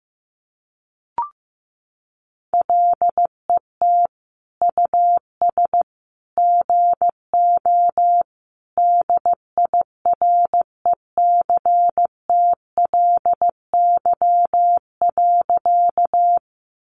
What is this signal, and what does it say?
Morse code, heard as a keyed tone, about 15 wpm, 700 Hz. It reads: LET US GO DIRECTLY.
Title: Sentences from Top 800 Words 15wpm